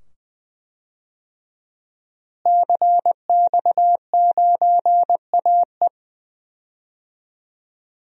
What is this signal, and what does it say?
Morse code, heard as a keyed tone, about 20 wpm, 700 Hz. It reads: CX9AE
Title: DXCC-100 Call Signs - 5 Characters - Repeated 2x - 20wpm